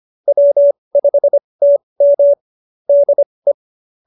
Morse code, heard as a keyed tone, about 25 wpm, spaced as for 20 wpm, 590 Hz. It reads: W5TM DE
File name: QSO316 20